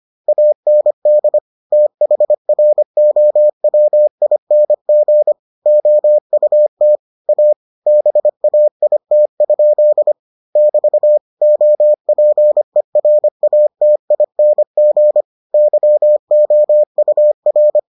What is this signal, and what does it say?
Morse code, heard as a keyed tone, about 25 wpm, 600 Hz. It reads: AND THROWING OUT A BAIT? <BT> OPERATING YOUR